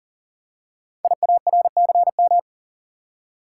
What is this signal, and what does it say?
Morse code, heard as a keyed tone, about 40 wpm, 700 Hz. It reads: IARCM
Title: Code Groups - 5 Letters - 40wpm